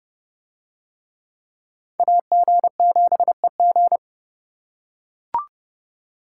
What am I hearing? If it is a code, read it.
Morse code, heard as a keyed tone, about 29 wpm, 700 Hz. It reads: AG7EG